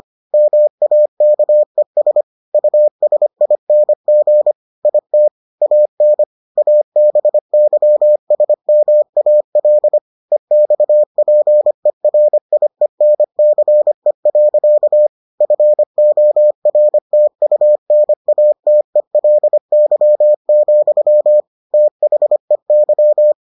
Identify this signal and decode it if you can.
Morse code, heard as a keyed tone, about 25 wpm, 600 Hz. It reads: MAKES USING IT AN ABYSMAL EXPERIENCE. FORTUNATELY, THEY